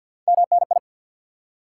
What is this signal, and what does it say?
Morse code, heard as a keyed tone, about 53 wpm, 700 Hz. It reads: MNI
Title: Top 90 QSO Elements 50wpm